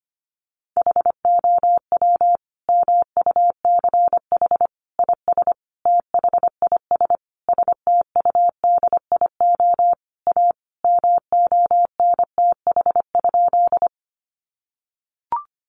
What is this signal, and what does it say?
Morse code, heard as a keyed tone, about 25 wpm, 700 Hz. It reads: HOW MUCH IS THIS STUDIO A MONTH?
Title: Sentences from Top 1200 Words 25wpm